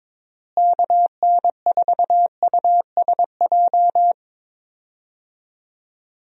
Morse code, heard as a keyed tone, about 22 wpm, 700 Hz. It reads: KN4USJ
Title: USA Call Signs - 6 Characters - Repeated 3x - 22wpm